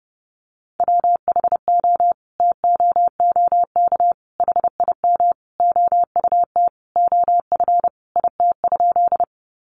Morse code, heard as a keyed tone, about 30 wpm, 700 Hz. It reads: WHO TOOK HIM OUT OF IT?
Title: Sentences from Top 300 Words 30wpm